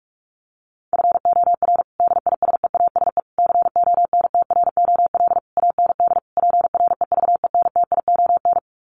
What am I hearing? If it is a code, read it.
Morse code, heard as a keyed tone, about 44 wpm, 700 Hz. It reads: FOR DISEASE CONTROL AND PREVENTION
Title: Sets of 5 Words Encore2 45wpm